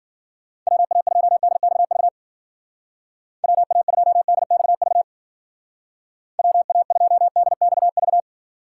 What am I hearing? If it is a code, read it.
Morse code, heard as a keyed tone, about 52 wpm, 700 Hz. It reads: WA2DXV WA2DXV WA2DXV